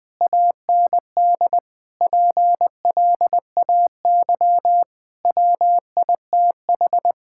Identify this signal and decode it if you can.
Morse code, heard as a keyed tone, about 20 wpm, 700 Hz. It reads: AND PLAY WITH
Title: Sentences from Top 1600 Words 20wpm